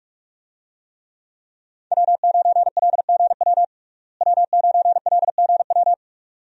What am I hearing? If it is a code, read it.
Morse code, heard as a keyed tone, about 46 wpm, 700 Hz. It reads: W9LGW W9LGW
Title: USA Call Signs - 5 Characters - Repeated 2x - 45wpm